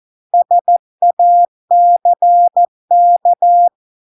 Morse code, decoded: SACK

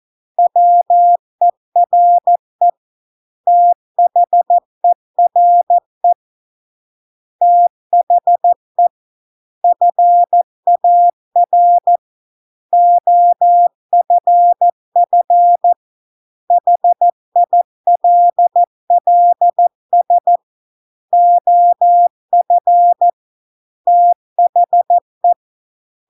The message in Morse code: WERE THERE THE FAR OFF HILLS OF THE

